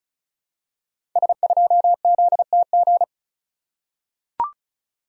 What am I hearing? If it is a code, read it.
Morse code, heard as a keyed tone, about 36 wpm, 700 Hz. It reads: S2ZTG
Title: Code Groups - 5 Letter:Numbers - 35wpm